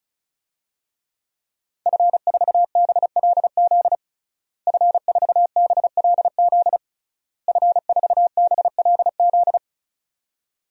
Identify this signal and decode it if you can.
Morse code, heard as a keyed tone, about 36 wpm, 700 Hz. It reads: F4BLZ F4BLZ F4BLZ
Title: DXCC-100 Call Signs - 5 Characters - Repeated 3x - 35wpm